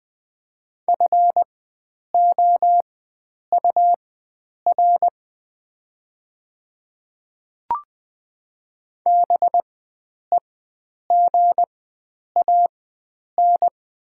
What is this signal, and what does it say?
Morse code, heard as a keyed tone, about 20 wpm, spaced as for 9 wpm, 700 Hz. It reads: FOUR BEGAN
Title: 4x Character Spacing - Top 500 Words - 20wpm